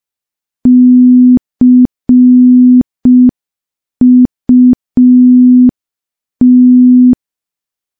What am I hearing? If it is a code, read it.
Morse code, heard as a keyed tone, about 5 wpm, 250 Hz. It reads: CUT
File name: morse / morse-training-05WPM